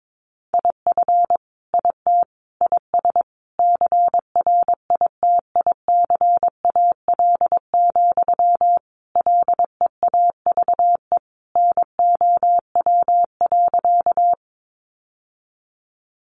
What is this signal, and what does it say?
Morse code, heard as a keyed tone, about 22 wpm, 700 Hz. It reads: IF IT IS CRITICAL, LEAVE NOW.
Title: Sentences from Top 1100 Words 22wpm